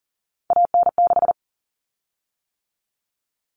Morse code, heard as a keyed tone, about 40 wpm, 700 Hz. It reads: AN6
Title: Common Cut Numbers - 3-Digit 40wpm